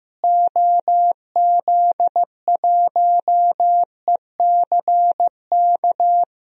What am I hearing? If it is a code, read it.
Morse code, heard as a keyed tone, about 15 wpm, 700 Hz. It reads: OZ1ECK
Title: DXCC-100 Call Signs - 6 Characters - Repeated 3x - 15wpm